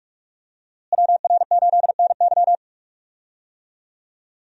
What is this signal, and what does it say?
Morse code, heard as a keyed tone, about 42 wpm, 700 Hz. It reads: WR8NY
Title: USA Call Signs - 5 Characters - Repeated 2x - 45wpm